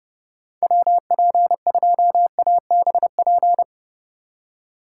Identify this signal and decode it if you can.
Morse code, heard as a keyed tone, about 30 wpm, 700 Hz. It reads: WP2ABP
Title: USA Call Signs - 6 Characters - 4x Character Spacing - 30wpm